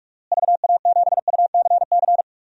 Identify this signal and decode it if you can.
Morse code, heard as a keyed tone, about 43 wpm, 700 Hz. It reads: VA7UCC
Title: DXCC-100 Call Signs - 6 Characters - Repeated 2x - 45wpm